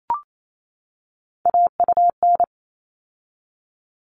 Morse code, heard as a keyed tone, about 28 wpm, 700 Hz. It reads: AUN
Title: All Cut Numbers - 3-Digit 28wpm